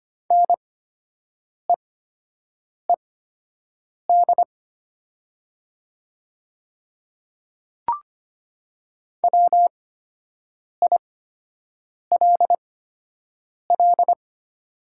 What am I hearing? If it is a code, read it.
Morse code, heard as a keyed tone, about 25 wpm, spaced as for 7 wpm, 700 Hz. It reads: NEED WILL
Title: 8x Character Spacing - Top 200 Words - 25wpm